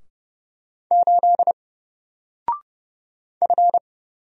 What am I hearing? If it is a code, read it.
Morse code, heard as a keyed tone, about 30 wpm, 700 Hz. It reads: 8 F
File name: Single-Letter-Number-30wpm